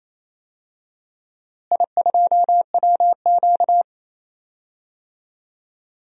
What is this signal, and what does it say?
Morse code, heard as a keyed tone, about 28 wpm, 700 Hz. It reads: I2WQ
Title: DXCC-100 Call Signs - 4 Characters - Repeated 2x - 28wpm